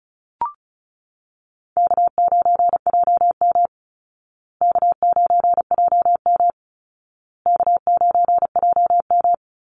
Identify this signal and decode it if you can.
Morse code, heard as a keyed tone, about 34 wpm, 700 Hz. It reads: K9JM K9JM K9JM
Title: Easy Call Signs Repeated 3x - 35wpm